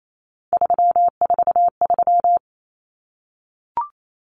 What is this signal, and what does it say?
Morse code, heard as a keyed tone, about 28 wpm, 700 Hz. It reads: 343
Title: Common Cut Numbers - 3-Digit 28wpm